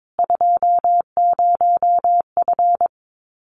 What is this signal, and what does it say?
Morse code, heard as a keyed tone, about 22 wpm, 700 Hz. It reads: 20F
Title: Common QSO Phrases Encore 22wpm